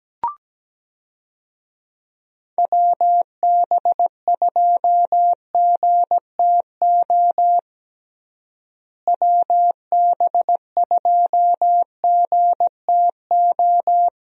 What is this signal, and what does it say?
Morse code, heard as a keyed tone, about 17 wpm, 700 Hz. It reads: WB2GTO WB2GTO
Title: USA Call Signs - 6 Characters - Repeated 3x - 17wpm